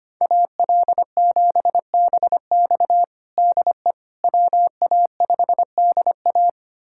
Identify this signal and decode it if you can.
Morse code, heard as a keyed tone, about 25 wpm, 700 Hz. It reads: AL7BX DE WA5DA
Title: USA Callsign DE Callsign - 5 Characters - 25wpm